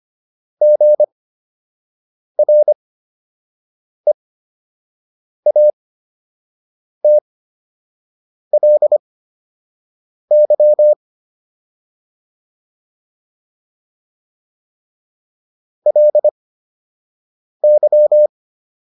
Morse code, heard as a keyed tone, about 25 wpm, spaced as for 6 wpm, 600 Hz. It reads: GREATLY LY